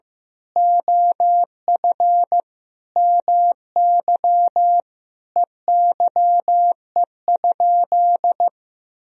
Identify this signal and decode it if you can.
Morse code, heard as a keyed tone, about 15 wpm, 700 Hz. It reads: OF MY EYE?
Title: Sentences from Top 500 Words and Abbreviations 15wpm